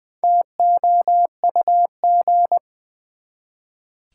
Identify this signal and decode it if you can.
Morse code, heard as a keyed tone, about 21 wpm, 700 Hz. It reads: TOUG
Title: Build A Word - 1 Letter At A Time - 5-Letter Words - 20wpm